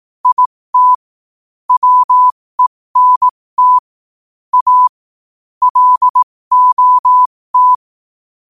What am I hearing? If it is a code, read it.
Morse code, heard as a keyed tone, about 18 wpm, spaced as for 15 wpm, 1000 Hz. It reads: IT WENT A LOT
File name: qotdfiles / 15wpm